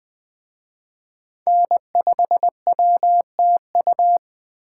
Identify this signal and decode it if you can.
Morse code, heard as a keyed tone, about 20 wpm, 700 Hz. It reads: N5WTU
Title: Speed Racing - USA 1x3 Call Signs 17wpm